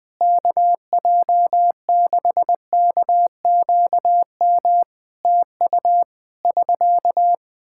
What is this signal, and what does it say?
Morse code, heard as a keyed tone, about 20 wpm, 700 Hz. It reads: KJ6KQM TU <SK>
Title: Practice QSO - Exchange 4 - Ending - 20wpm